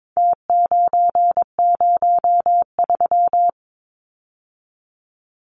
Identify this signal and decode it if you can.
Morse code, heard as a keyed tone, about 22 wpm, 700 Hz. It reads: T903
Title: Numbers - Contest Style - 4-Digit 22wpm